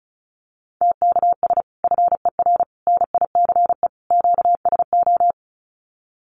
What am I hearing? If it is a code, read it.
Morse code, heard as a keyed tone, about 34 wpm, 700 Hz. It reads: TKS FER NICE QSO
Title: Common QSO Phrases 35wpm